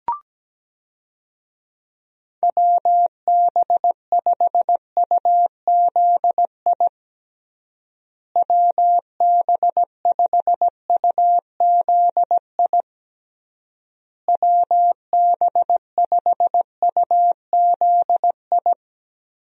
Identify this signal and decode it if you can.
Morse code, heard as a keyed tone, about 17 wpm, 700 Hz. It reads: WB5UZI WB5UZI WB5UZI
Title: USA Call Signs - 6 Characters - Repeated 3x - 17wpm